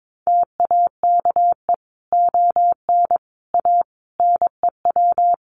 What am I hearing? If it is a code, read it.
Morse code, heard as a keyed tone, about 22 wpm, 700 Hz. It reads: TAKE ON A NEW
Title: Sentences from Top 1400 Words 22wpm